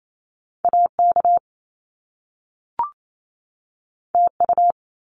Morse code, heard as a keyed tone, about 28 wpm, 700 Hz. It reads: AK TU
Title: Top Two Letter Combinations in Words 28wpm